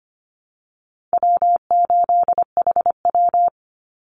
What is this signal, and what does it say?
Morse code, heard as a keyed tone, about 25 wpm, 700 Hz. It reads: W8HW
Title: Easy Call Signs Repeated 2x - 25wpm